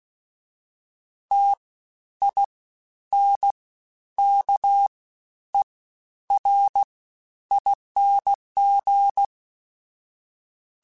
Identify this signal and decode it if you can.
Morse code, heard as a keyed tone, about 16 wpm, spaced as for 10 wpm, 800 Hz. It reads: TINKERING
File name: Special INGs-mp3s / Special INGs-mp3s / SlowToFaster ing @15